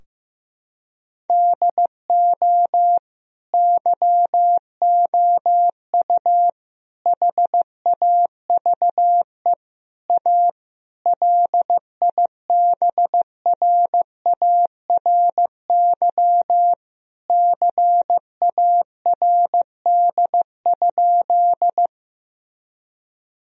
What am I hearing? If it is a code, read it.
Morse code, heard as a keyed tone, about 15 wpm, 700 Hz. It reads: DO YOU HAVE A LIBRARY CARD?
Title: Sentences from Top 1000 Words - Review Part 4 - 15wpm